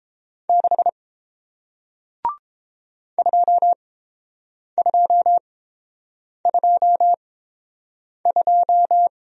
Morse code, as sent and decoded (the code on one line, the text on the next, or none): -.... / ..--- ..--- ..--- ..---
6 2222